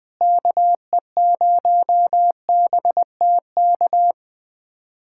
Morse code, heard as a keyed tone, about 21 wpm, 700 Hz. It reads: KE0BTK
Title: USA Callsign DE Callsign - 6 Characters - 20wpm